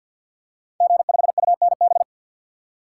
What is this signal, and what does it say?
Morse code, heard as a keyed tone, about 50 wpm, 700 Hz. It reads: G5UNB
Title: Code Groups - 5 Letter:Numbers - 50wpm